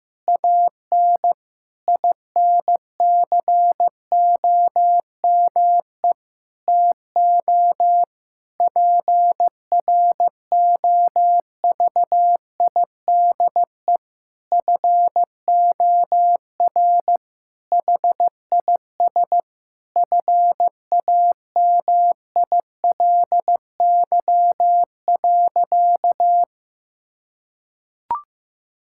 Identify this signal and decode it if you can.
Morse code, heard as a keyed tone, about 15 wpm, 700 Hz. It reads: AN INCOME TO PROVIDE FOR HIS FAMILY.
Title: Sentences from Top 1000 Words - Review Part 5 - 15wpm